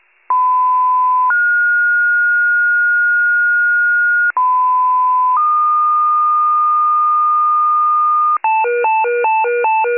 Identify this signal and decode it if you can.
Two-tone page, heard as a one-second tone, then a three-second tone, about 1010 then 1430 Hz, about 1010 then 1220 Hz, with noise. It is heard from 0.3 to 8.4 s.